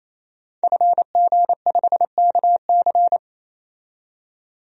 Morse code, heard as a keyed tone, about 27 wpm, 700 Hz. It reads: FG5KC
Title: DXCC-100 Call Signs - 5 Characters - 28wpm